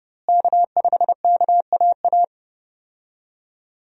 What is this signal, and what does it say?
Morse code, heard as a keyed tone, about 30 wpm, 700 Hz. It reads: K5KAA
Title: USA Call Signs - 5 Characters - Repeated 2x - 30wpm